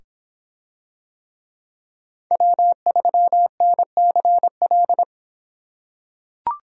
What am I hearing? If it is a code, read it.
Morse code, heard as a keyed tone, about 25 wpm, 700 Hz. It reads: W3NCL